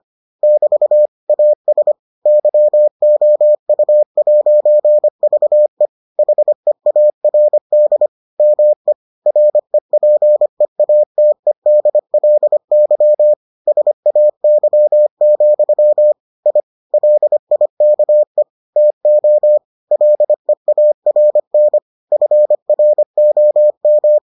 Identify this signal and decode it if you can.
Morse code, heard as a keyed tone, about 25 wpm, 600 Hz. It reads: <BT> AS YOU'VE HEARD ME REPEATEDLY SAY, I LIKE TO LEARN FROM